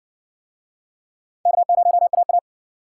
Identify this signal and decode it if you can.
Morse code, heard as a keyed tone, about 60 wpm, 700 Hz. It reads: K0AA